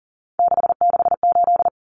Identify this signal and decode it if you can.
Morse code, heard as a keyed tone, about 40 wpm, 700 Hz. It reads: 668
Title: Common Cut Numbers - 3-Digit 40wpm